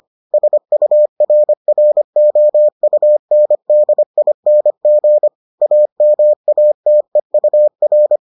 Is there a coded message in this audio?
Yes: Morse code from 0.3 to 8.2 s, heard as a keyed tone, about 25 wpm, 600 Hz. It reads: SURROUNDING AMATEUR